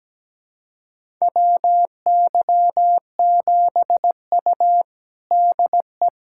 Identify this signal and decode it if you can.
Morse code, heard as a keyed tone, about 17 wpm, 700 Hz. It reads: WY7U DE
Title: USA Callsign DE Callsign - 4 Characters - 17wpm